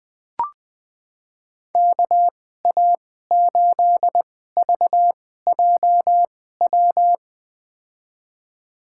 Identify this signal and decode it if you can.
Morse code, heard as a keyed tone, about 20 wpm, spaced as for 14 wpm, 700 Hz. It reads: KA8VJW
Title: USA Call Signs - 6 Characters - 2x Character Spacing - 20wpm